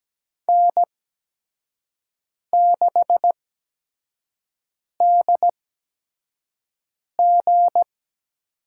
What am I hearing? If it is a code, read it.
Morse code, heard as a keyed tone, about 17 wpm, spaced as for 5 wpm, 700 Hz. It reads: N6DG
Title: USA Call Signs - 5 Characters - 8x Character Spacing - 17wpm